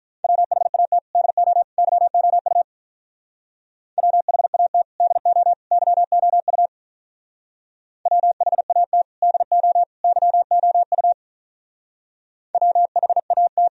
Morse code, heard as a keyed tone, about 45 wpm, 700 Hz. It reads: WHAT DO YOU WHAT DO YOU WHAT DO YOU WHAT